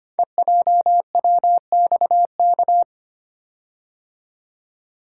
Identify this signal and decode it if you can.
Morse code, heard as a keyed tone, about 25 wpm, 700 Hz. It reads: EJWXK
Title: Code Groups - 5 Letters - 25wpm